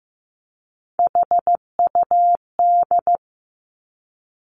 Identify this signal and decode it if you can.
Morse code, heard as a keyed tone, about 15 wpm, 700 Hz. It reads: HUD
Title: Three Challenging Letter-Number Combinations 15wpm